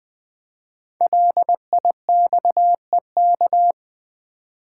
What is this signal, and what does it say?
Morse code, heard as a keyed tone, about 20 wpm, 700 Hz. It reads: LIXEK
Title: Code Groups - 5 Letter:Numbers - 20wpm